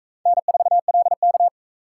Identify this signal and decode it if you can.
Morse code, heard as a keyed tone, about 41 wpm, 700 Hz. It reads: N4LK